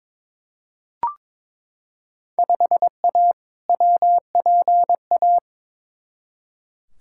Morse code, heard as a keyed tone, about 22 wpm, 700 Hz. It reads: 5A WPA
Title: Field Day Exchanges Encore - 22wpm